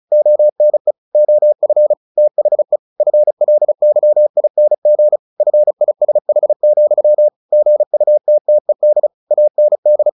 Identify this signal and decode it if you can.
Morse code, heard as a keyed tone, about 34 wpm, 600 Hz. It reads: ONE OF THE FLYING FISH, GUTTED AND
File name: the old man and the sea 2 35wpm